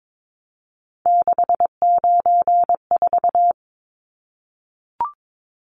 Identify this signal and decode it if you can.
Morse code, heard as a keyed tone, about 22 wpm, 700 Hz. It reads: 694
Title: Numbers - 3-Digit 22wpm